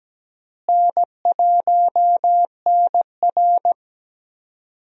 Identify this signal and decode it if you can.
Morse code, heard as a keyed tone, about 17 wpm, 700 Hz. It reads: N1NR
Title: USA Call Signs - 4 Characters - Repeated 2x - 17wpm